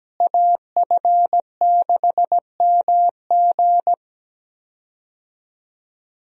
Morse code, heard as a keyed tone, about 17 wpm, 700 Hz. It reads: AF6MG
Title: USA Call Signs - 5 Characters - Repeated 3x - 17wpm